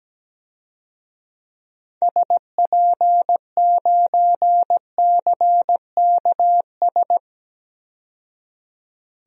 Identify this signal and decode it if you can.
Morse code, heard as a keyed tone, about 17 wpm, 700 Hz. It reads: SP9CKS
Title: DXCC-100 Call Signs - 6 Characters - 17wpm